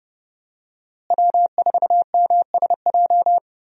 Morse code, heard as a keyed tone, about 31 wpm, 700 Hz. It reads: W4MSJ
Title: USA Call Signs - 5 Characters - Repeated 2x - 30wpm